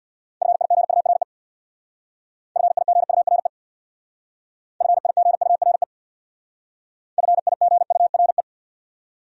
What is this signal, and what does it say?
Morse code, heard as a keyed tone, about 63 wpm, 700 Hz. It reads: FIGURE FIGURE FIGURE FIGURE